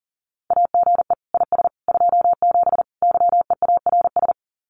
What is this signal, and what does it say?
Morse code, heard as a keyed tone, about 40 wpm, 700 Hz. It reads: AGE IS 27 YEARS
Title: Common QSO Phrases Encore 40wpm